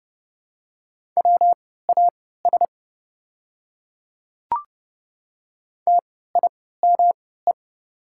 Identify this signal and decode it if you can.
Morse code, heard as a keyed tone, about 30 wpm, spaced as for 17 wpm, 700 Hz. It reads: WAS TIME
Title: 3x Character Spacing - Top 100 Words - 30wpm